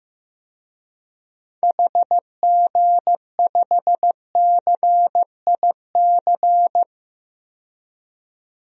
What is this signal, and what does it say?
Morse code, heard as a keyed tone, about 16 wpm, 700 Hz. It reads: HG5CIC
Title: DXCC-100 Call Signs - 6 Characters - 15wpm